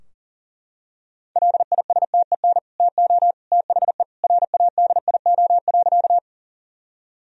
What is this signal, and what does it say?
Morse code, heard as a keyed tone, about 40 wpm, 700 Hz. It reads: LISTEN TO THE RADIO.
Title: Sentences from Top 1000 Words - Review Part 3 - 40wpm